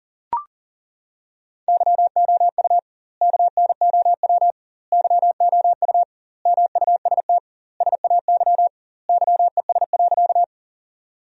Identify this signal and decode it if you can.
Morse code, heard as a keyed tone, about 40 wpm, 700 Hz. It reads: YOU KNOW YOU MUST SAY YES.
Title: 2x Word-Spacing - Sentences from Top 500 Words - 40wpm